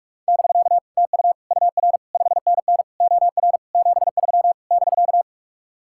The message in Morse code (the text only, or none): <BK> TU UR 5NN OR 73 <BK>